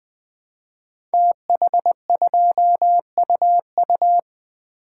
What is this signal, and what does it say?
Morse code, heard as a keyed tone, about 20 wpm, 700 Hz. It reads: TH2UU
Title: Code Groups - 5 Letter:Numbers - 20wpm